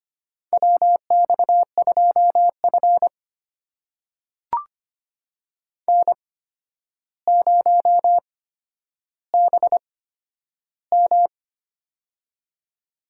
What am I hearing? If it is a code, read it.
Morse code, heard as a keyed tone, about 25 wpm, spaced as for 11 wpm, 700 Hz. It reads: WX2F N0BM
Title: USA Call Signs - 4 Characters - 8x Character Spacing - 25wpm